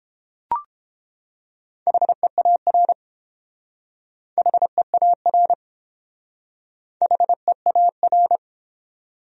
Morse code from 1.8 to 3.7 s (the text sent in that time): HEAR